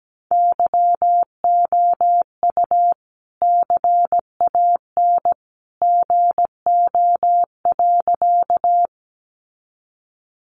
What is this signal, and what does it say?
Morse code, heard as a keyed tone, about 17 wpm, 700 Hz. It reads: YOU CAN GO.
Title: Sentences from Top 100 Words 17wpm